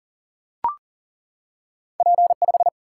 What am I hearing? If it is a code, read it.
Morse code, heard as a keyed tone, about 39 wpm, 700 Hz. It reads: P 5
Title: Single-Letter Single-Number 40wpm